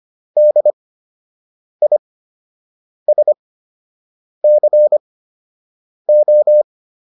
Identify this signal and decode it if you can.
Morse code, heard as a keyed tone, about 25 wpm, spaced as for 7 wpm, 600 Hz. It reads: DISCO